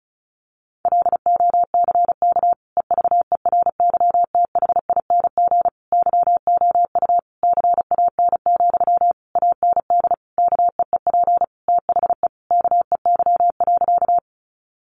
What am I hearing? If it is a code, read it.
Morse code, heard as a keyed tone, about 34 wpm, 700 Hz. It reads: LOCK EVERYTHING YOU CAN, AND KEEP THE KEY.
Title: Sentences from Top 1300 Words 35wpm